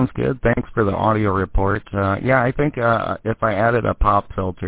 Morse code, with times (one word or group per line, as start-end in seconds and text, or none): none